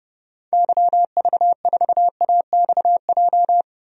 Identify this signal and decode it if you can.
Morse code, heard as a keyed tone, about 30 wpm, 700 Hz. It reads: YV4AXJ
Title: DXCC-100 Call Signs - 6 Characters - 30wpm